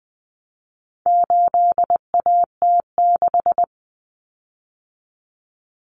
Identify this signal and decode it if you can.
Morse code, heard as a keyed tone, about 20 wpm, 700 Hz. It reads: 8AT6
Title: Common Cut Numbers - 4-Digit 20wpm